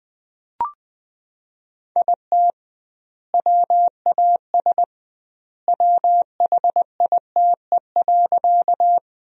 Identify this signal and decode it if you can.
Morse code, heard as a keyed tone, about 20 wpm, 700 Hz. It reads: IT WAS WHITE.
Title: 2x Word-Spacing - Sentences from Top 300 Words - 20wpm